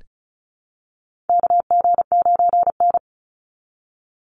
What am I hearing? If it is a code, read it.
Morse code, heard as a keyed tone, about 34 wpm, 700 Hz. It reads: KG9N
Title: Call Signs Repeated 3x - 35wpm